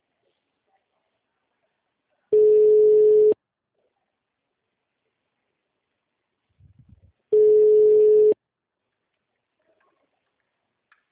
Ringback tone (Europe, 425 Hz).